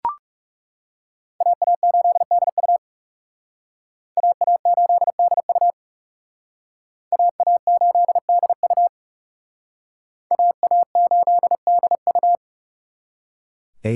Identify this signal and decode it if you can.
Morse code, heard as a keyed tone, about 36 wpm, 700 Hz. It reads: AA8DU AA8DU AA8DU AA8DU
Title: Speed Racing - USA 2x2 Call Signs 30wpm